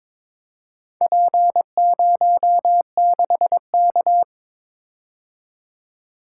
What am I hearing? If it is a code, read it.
Morse code, heard as a keyed tone, about 22 wpm, 700 Hz. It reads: P06K
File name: practice / 22wpm / Four Letter-Number Combinations - 22wpm